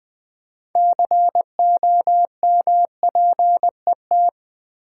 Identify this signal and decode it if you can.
Morse code, heard as a keyed tone, about 20 wpm, 700 Hz. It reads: COMPET
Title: Build A Word - 3 Letters At A Time - 9-Letter Words - 20wpm